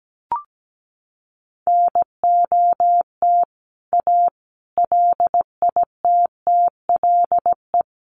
Morse code, heard as a keyed tone, about 17 wpm, 700 Hz. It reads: NOT A LITTLE
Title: A Sound Of Thunder 17wpm